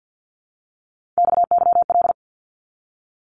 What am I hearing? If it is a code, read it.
Morse code, heard as a keyed tone, about 50 wpm, 700 Hz. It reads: XYL
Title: Abbreviations 50wpm